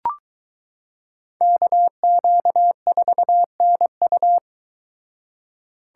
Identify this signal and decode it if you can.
Morse code, heard as a keyed tone, about 22 wpm, 700 Hz. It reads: KQ4NU